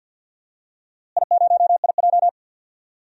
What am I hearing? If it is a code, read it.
Morse code, heard as a keyed tone, about 53 wpm, 700 Hz. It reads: I0IJ
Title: DXCC-100 Call Signs - 4 Characters - Repeated 2x - 50wpm